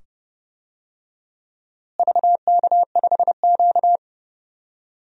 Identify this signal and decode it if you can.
Morse code, heard as a keyed tone, about 30 wpm, 700 Hz. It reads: VK5Q
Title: DXCC-100 Call Signs - 4 Characters - Repeated 2x - 30wpm